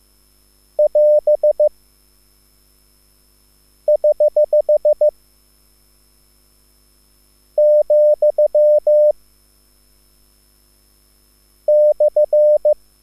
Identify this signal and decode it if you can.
Morse code, heard as a keyed tone, about 15 wpm, 600 Hz. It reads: <AS> <HH> , /
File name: QSL / CW lesson